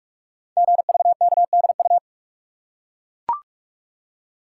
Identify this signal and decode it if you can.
Morse code, heard as a keyed tone, about 44 wpm, 700 Hz. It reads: GVKDU
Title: Code Groups - 5 Letter:Numbers - 45wpm